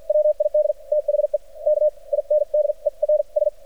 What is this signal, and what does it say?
Morse code, heard as a keyed tone, about 47 wpm, spaced as for 38 wpm, 600 Hz. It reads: WID THE KINDERS
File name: archived / CRSnet